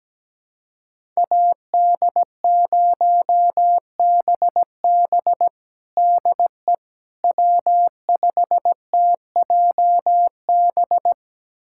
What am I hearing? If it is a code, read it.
Morse code, heard as a keyed tone, about 17 wpm, 700 Hz. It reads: AD0BB DE W5TJB
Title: USA Callsign DE Callsign - 5 Characters - 17wpm